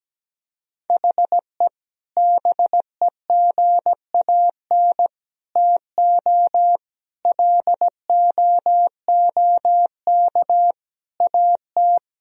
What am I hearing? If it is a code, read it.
Morse code, heard as a keyed tone, about 17 wpm, 700 Hz. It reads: HE BEGAN TO LOOK AT
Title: Sentences from Top 1000 Words - Review Part 1 - 17wpm